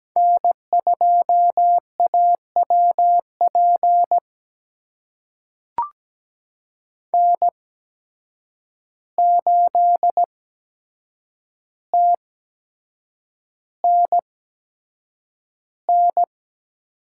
Morse code, heard as a keyed tone, about 17 wpm, spaced as for 7 wpm, 700 Hz. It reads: N2AWP N8TNN
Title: USA Call Signs - 5 Characters - 8x Character Spacing - 17wpm